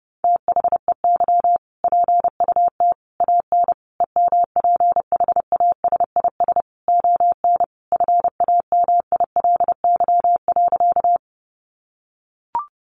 Morse code, heard as a keyed tone, about 30 wpm, 700 Hz. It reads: THEY PUT AN EMPHASIS ON FAMILY.
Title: Sentences from Top 1400 Words 30wpm